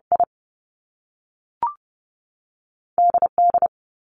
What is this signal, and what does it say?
Morse code, heard as a keyed tone, about 30 wpm, 700 Hz. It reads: I D D
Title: Top Two Letter Combinations in Words 30wpm